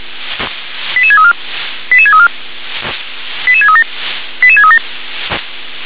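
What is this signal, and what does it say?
Five-tone selective call (ZVEI), heard as quick runs of five tones, tones of about 70 ms. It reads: 88634 88634 88637 88637
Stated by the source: ZVEI five-tone call received via SDR.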